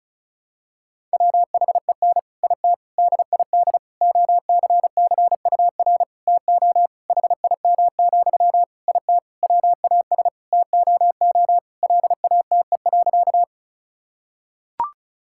Morse code, heard as a keyed tone, about 36 wpm, 700 Hz. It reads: WHEN IT DID OCCUR TO HIM, IT WAS TOO LATE.
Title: Sentences from Top 1000 Words - Review Part 6 - 35wpm